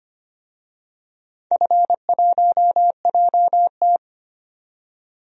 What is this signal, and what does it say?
Morse code, heard as a keyed tone, about 25 wpm, 700 Hz. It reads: F1JT
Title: DXCC-100 Call Signs - 4 Characters - 25wpm